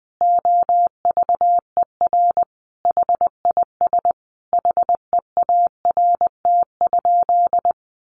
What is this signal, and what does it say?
Morse code, heard as a keyed tone, about 20 wpm, 700 Hz. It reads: OVER HIS HEART?
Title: Sentences from Top 1200 Words 20wpm